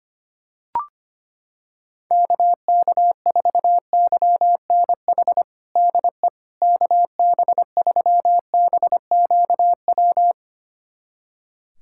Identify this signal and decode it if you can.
Morse code, heard as a keyed tone, about 25 wpm, 700 Hz. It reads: KK4YNH DE KB3BQW
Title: USA Callsign DE Callsign - 6 Characters - 25wpm